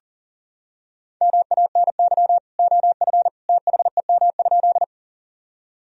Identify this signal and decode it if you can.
Morse code, heard as a keyed tone, about 37 wpm, 700 Hz. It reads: MANY OF THEM?
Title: Sentences from Top 1000 Words - Review Part 1 - 40wpm